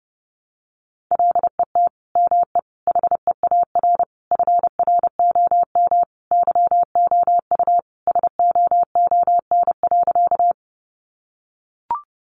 Morse code, heard as a keyed tone, about 30 wpm, 700 Hz. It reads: LET ME HEAR FROM YOU SOON.
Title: Sentences from Top 400 Words 30wpm